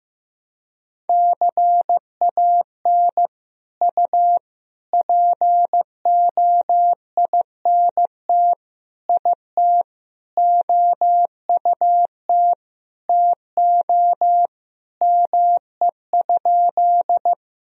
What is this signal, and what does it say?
Morse code, heard as a keyed tone, about 15 wpm, 700 Hz. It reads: CAN U POINT IT OUT TO ME?
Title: Sentences from Top 200 Words and Abbreviations 15wpm